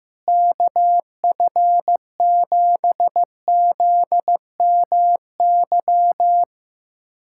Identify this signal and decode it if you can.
Morse code, heard as a keyed tone, about 15 wpm, 700 Hz. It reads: KF7ZMY
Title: USA Callsign DE Callsign 15wpm